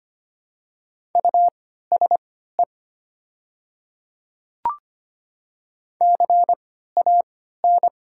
Morse code, heard as a keyed tone, about 25 wpm, spaced as for 14 wpm, 700 Hz. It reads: USE CAN
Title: 3x Character Spacing - Top 500 Words - 25wpm